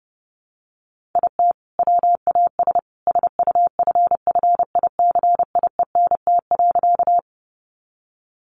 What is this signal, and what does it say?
Morse code, heard as a keyed tone, about 30 wpm, 700 Hz. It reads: IT WAS SUFFICIENT.
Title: Sentences from Top 1400 Words 30wpm